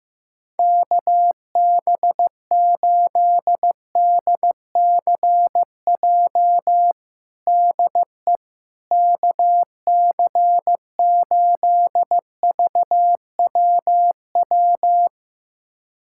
Morse code, heard as a keyed tone, about 16 wpm, 700 Hz. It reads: KB8DCJ DE KC8VWW
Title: USA Callsign DE Callsign - 6 Characters - 15wpm